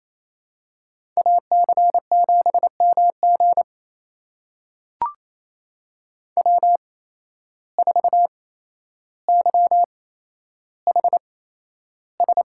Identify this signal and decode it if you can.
Morse code, heard as a keyed tone, about 28 wpm, spaced as for 12 wpm, 700 Hz. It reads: AC7MG W4YHS
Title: USA Call Signs - 8x Character Spacing - 28wpm